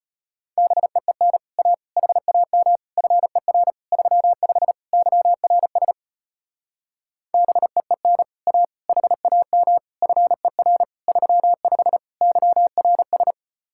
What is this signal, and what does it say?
Morse code, heard as a keyed tone, about 33 wpm, 700 Hz. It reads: BEEN A HAM FER 35 YRS BEEN A HAM FER 35 YRS